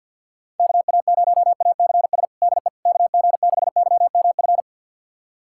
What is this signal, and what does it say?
Morse code, heard as a keyed tone, about 48 wpm, 700 Hz. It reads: KA0ACS DE KG6YMF
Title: USA Callsign DE Callsign 50wpm